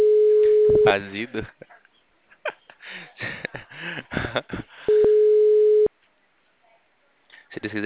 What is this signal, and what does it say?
Ringback tone (Europe, 425 Hz), with speech.
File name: out- -2016- - - .79420